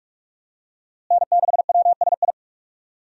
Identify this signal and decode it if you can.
Morse code, heard as a keyed tone, about 41 wpm, 700 Hz. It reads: N6WSI
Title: USA Call Signs - 5 Characters - Repeated 2x - 45wpm